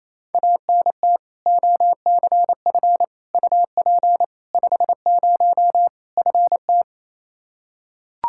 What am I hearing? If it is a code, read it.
Morse code, heard as a keyed tone, about 28 wpm, 700 Hz. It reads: ANT OCF UP 50 FT
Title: Practice QSO - Exchange 2 - Info - 28wpm